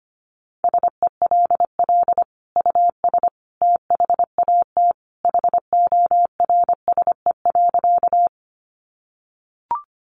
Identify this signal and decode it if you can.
Morse code, heard as a keyed tone, about 25 wpm, 700 Hz. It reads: SELL US THAT HORSE.
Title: Sentences from Top 1100 Words 25wpm